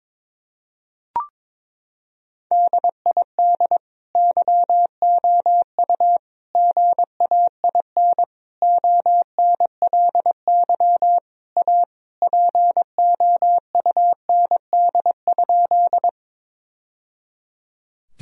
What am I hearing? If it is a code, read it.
Morse code, heard as a keyed tone, about 22 wpm, 700 Hz. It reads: DID YOU GAIN ONLY A POUND?
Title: Sentences from Top 1000 Words - Review Part 2 - 22wpm